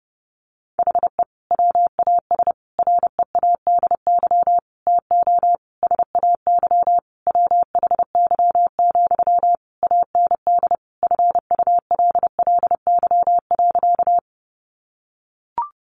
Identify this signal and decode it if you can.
Morse code, heard as a keyed tone, about 30 wpm, 700 Hz. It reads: HE WAS READY TO SAY WHY, AND FULLY.
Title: Sentences from Top 1300 Words 30wpm